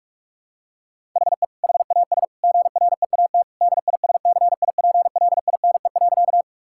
Morse code, heard as a keyed tone, about 41 wpm, 700 Hz. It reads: HE HAS GREAT DISCIPLINE.
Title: Sentences from Top 1000 Words - Review Part 6 - 45wpm